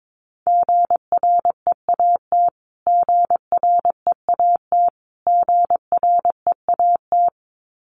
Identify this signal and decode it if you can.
Morse code, heard as a keyed tone, about 22 wpm, 700 Hz. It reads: GREAT GREAT GREAT
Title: Top 500 Words Repeated 22wpm